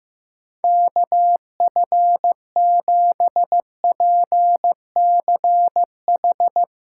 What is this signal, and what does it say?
Morse code, heard as a keyed tone, about 15 wpm, 700 Hz. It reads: KF7PCH
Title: USA Call Signs - 6 Characters - 2x Character Spacing - 15wpm